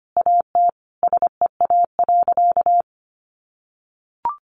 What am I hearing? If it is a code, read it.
Morse code, heard as a keyed tone, about 25 wpm, 700 Hz. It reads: AT SEA.
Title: Sentences from Top 1400 Words 25wpm